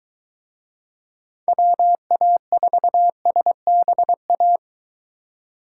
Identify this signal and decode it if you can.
Morse code, heard as a keyed tone, about 23 wpm, 700 Hz. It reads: WA4SBA